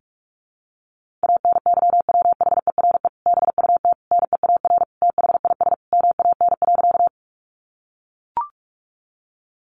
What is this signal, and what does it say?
Morse code, heard as a keyed tone, about 44 wpm, 700 Hz. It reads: ANYWHERE BUT NEAR THIS MAN.
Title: Sentences from Top 1400 Words 45wpm